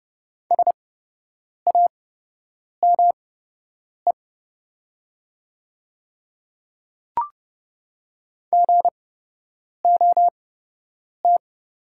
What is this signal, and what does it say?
Morse code, heard as a keyed tone, about 30 wpm, spaced as for 8 wpm, 700 Hz. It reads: SAME GOT
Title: 8x Character Spacing - Top 300 Words - 30wpm